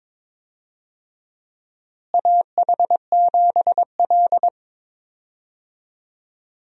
Morse code, heard as a keyed tone, about 22 wpm, 700 Hz. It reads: AH7L
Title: USA Call Signs - 4 Characters - Repeated 3x - 22wpm